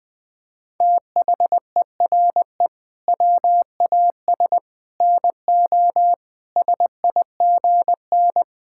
Morse code, heard as a keyed tone, about 20 wpm, 700 Hz. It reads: THERE WAS NO SIGN
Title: Sentences from Top 1000 Words - Review Part 6 - 20wpm